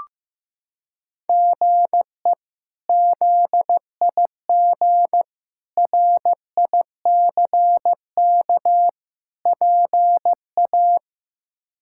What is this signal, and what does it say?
Morse code, heard as a keyed tone, about 15 wpm, 700 Hz. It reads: GE ZIG RICK PA
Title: K1USN SST QSO Practice - 15wpm